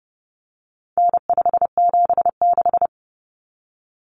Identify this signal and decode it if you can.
Morse code, heard as a keyed tone, about 30 wpm, 700 Hz. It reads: N576
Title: Common Cut Numbers - 4-Digit 30wpm